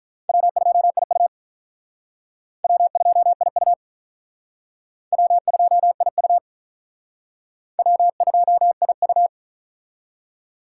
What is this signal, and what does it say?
Morse code, heard as a keyed tone, about 43 wpm, 700 Hz. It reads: W2IU W2IU W2IU W2IU
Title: Speed Racing - USA Call Signs 35wpm